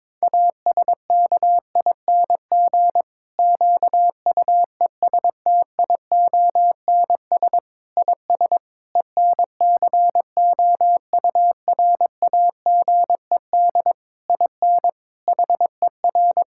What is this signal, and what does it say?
Morse code, heard as a keyed tone, about 22 wpm, 700 Hz. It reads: ASKING QUESTIONS IS ENCOURAGED IN HER
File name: Difficult Sentences with Hints - 22wpm